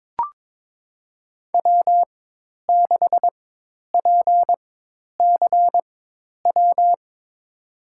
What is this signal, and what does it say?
Morse code, heard as a keyed tone, about 21 wpm, spaced as for 10 wpm, 700 Hz. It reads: W6PCW
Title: USA Call Signs - 5 Characters - 4x Character Spacing - 22wpm